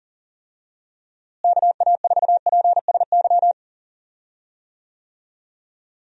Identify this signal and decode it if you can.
Morse code, heard as a keyed tone, about 40 wpm, 700 Hz. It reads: KA4PSY
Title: USA Call Signs - 6 Characters - Repeated 2x - 40wpm